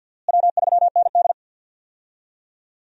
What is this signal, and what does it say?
Morse code, heard as a keyed tone, about 52 wpm, 700 Hz. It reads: W3ND
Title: USA Call Signs - 4 Characters - Repeated 3x - 50wpm